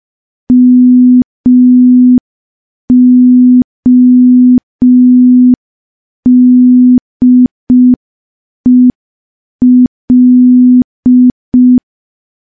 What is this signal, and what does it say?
Morse code, heard as a keyed tone, about 5 wpm, 250 Hz. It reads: MODEL
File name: morse / morse-training-05WPM